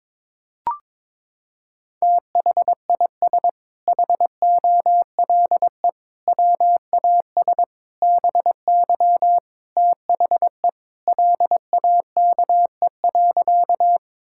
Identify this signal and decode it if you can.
Morse code, heard as a keyed tone, about 22 wpm, 700 Hz. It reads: THIS HOLE WAS BY THE LAKE.
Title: Sentences from Top 1600 Words 22wpm